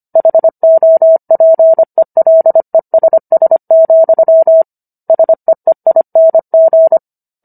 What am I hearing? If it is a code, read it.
Morse code, heard as a keyed tone, about 25 wpm, 650 Hz. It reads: HOPELESS, SEEING